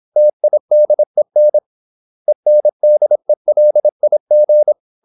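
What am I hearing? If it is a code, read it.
Morse code, heard as a keyed tone, about 26 wpm, 600 Hz. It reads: TIDEN ENDELIG
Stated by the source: High-speed (QRQ) Morse at 26 wpm.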